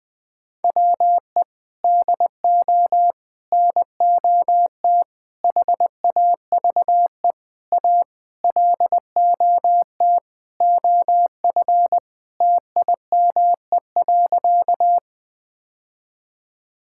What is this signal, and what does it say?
Morse code, heard as a keyed tone, about 21 wpm, 700 Hz. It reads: WE DO NOT HAVE A LOT OF TIME.
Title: Sentences with 5-Word Phrases - 20wpm